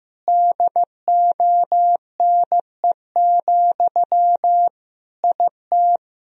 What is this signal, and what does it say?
Morse code, heard as a keyed tone, about 15 wpm, 700 Hz. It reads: DONE, IT
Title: Sentences from Top 1000 Words - Review Part 2 - 15wpm